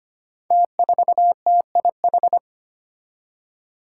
Morse code, heard as a keyed tone, about 25 wpm, 700 Hz. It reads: T4TIH
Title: Five Challenging Letter-Number Combinations 25wpm